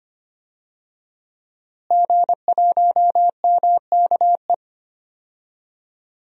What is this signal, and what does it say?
Morse code, heard as a keyed tone, about 25 wpm, 700 Hz. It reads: G1MKE
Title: DXCC-100 Call Signs - 5 Characters - Repeated 3x - 25wpm